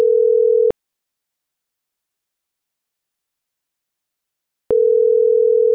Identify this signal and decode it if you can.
Ringback tone (North America), 440 + 480 Hz.